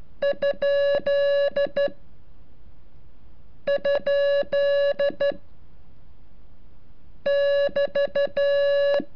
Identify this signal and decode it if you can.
Morse code, heard as a keyed tone, about 12 wpm, 570 Hz. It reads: ? ? <BT>
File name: curso / ejercicio 34